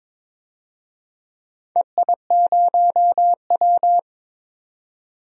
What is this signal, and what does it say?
Morse code, heard as a keyed tone, about 22 wpm, 700 Hz. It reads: EI0W
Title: DXCC-100 Call Signs - 4 Characters - 22wpm